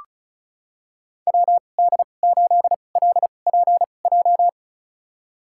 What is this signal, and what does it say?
Morse code, heard as a keyed tone, about 36 wpm, spaced as for 25 wpm, 690 Hz. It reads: WD8LPJ
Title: USA Call Signs - 6 Characters - 2x Character Spacing - 35wpm